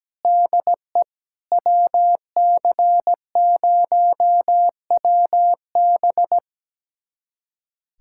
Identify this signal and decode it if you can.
Morse code, heard as a keyed tone, about 17 wpm, 700 Hz. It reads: DE WC0WB
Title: USA Callsign DE Callsign - 5 Characters - 17wpm